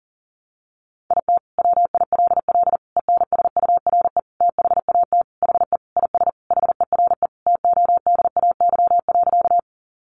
Morse code, heard as a keyed tone, about 40 wpm, 700 Hz. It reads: IT WILL ENSURE THAT HE IS HERE TODAY.
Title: Sentences from Top 1100 Words 40wpm